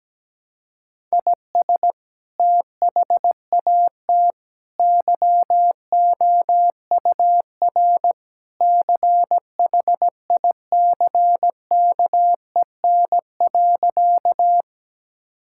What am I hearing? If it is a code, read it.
Morse code, heard as a keyed tone, about 17 wpm, 700 Hz. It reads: IS THAT YOUR CHICKEN.
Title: Sentences from Top 1000 Words - Review Part 5 - 17wpm